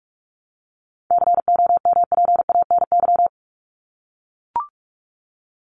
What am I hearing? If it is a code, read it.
Morse code, heard as a keyed tone, about 44 wpm, 700 Hz. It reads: COMPANY
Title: Top 501-750 Words 45wpm